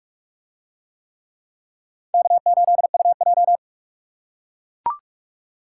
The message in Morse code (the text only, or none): K8UJ